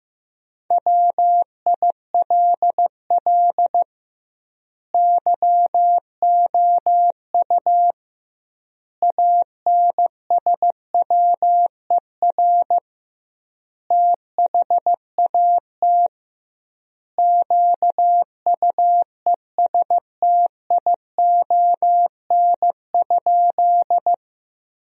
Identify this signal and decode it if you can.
Morse code, heard as a keyed tone, about 16 wpm, 700 Hz. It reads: WILL YOU ANSWER THAT QUESTION?
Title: 2x Word-Spacing - Sentences from Top 400 Words - 15wpm